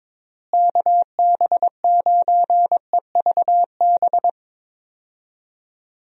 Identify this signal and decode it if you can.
Morse code, heard as a keyed tone, about 22 wpm, 700 Hz. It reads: KB9EVB
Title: USA Call Signs - 6 Characters - Repeated 3x - 22wpm